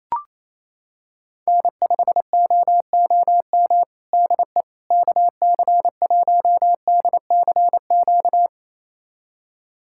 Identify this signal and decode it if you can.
Morse code, heard as a keyed tone, about 28 wpm, 700 Hz. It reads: N5OOM DE KC1DCQ
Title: USA Callsign DE Callsign 28wpm